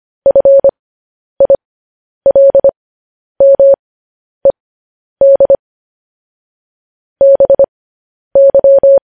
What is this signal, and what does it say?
Morse code, heard as a keyed tone, about 25 wpm, spaced as for 10 wpm, 550 Hz. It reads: FILMED BY
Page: News Headlines in Morse Code at 10 WPM